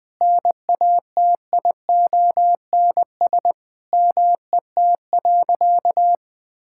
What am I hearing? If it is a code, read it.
Morse code, heard as a keyed tone, about 20 wpm, 700 Hz. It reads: NATIONS MET.
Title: Sentences with 2-Word Phrases - Spoken Beginning - 20wpm